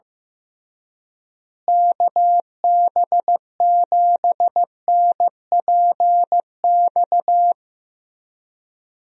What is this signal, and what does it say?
Morse code, heard as a keyed tone, about 15 wpm, 700 Hz. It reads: KB7NPX
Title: USA Call Signs - 6 Characters - Repeated 3x - 15wpm